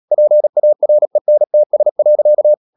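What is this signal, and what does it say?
Morse code, heard as a keyed tone, about 37 wpm, 600 Hz. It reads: PARENTS.
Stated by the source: Morse practice at 36 wpm, text of Ken Follett's novel Never, from CW med Gnister.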